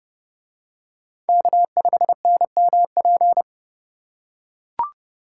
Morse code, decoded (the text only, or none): K5NMP